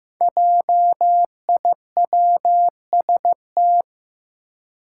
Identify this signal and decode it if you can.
Morse code, heard as a keyed tone, about 15 wpm, 700 Hz. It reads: JIWST
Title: Five Challenging Letter-Number Combinations 15wpm